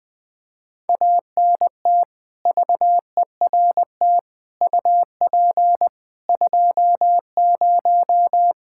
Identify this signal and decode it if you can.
Morse code, heard as a keyed tone, about 21 wpm, 700 Hz. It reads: ANT VERT UP 20
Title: Practice QSO - Exchange 2 - Info - 20wpm